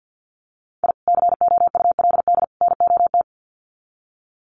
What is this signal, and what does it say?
Morse code, heard as a keyed tone, about 50 wpm, 700 Hz. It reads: I COULD NOT
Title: Sets of 3 Words 50wpm